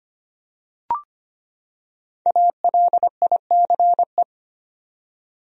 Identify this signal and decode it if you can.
Morse code, heard as a keyed tone, about 25 wpm, 700 Hz. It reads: ALICE
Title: USA Common Names - Max 5 Letters - 25wpm